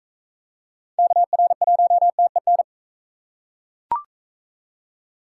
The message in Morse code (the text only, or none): KR1TEN